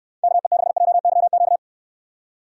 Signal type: Morse code

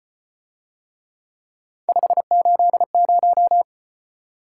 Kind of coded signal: Morse code

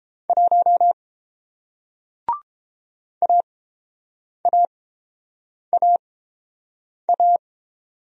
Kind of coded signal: Morse code

